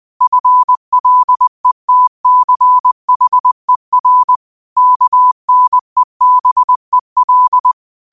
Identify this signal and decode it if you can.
Morse code, heard as a keyed tone, about 20 wpm, 1000 Hz. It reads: FLETCHER KNEBEL
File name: qotdfiles / 20wpm